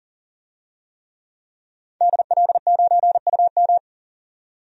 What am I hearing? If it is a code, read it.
Morse code, heard as a keyed tone, about 39 wpm, 700 Hz. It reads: DL9UM
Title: DXCC-100 Call Signs - 5 Characters - 40wpm